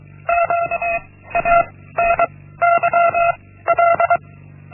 Morse code, heard as a keyed tone, about 23 wpm, 680 Hz. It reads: Q A N Y L
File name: Linksmakalnis-1